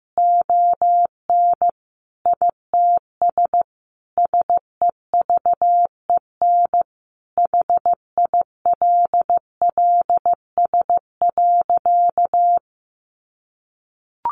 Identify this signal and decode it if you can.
Morse code, heard as a keyed tone, about 15 wpm, 700 Hz. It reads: ON ITS SEVEN HILLS.
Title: A Sound Of Thunder 15wpm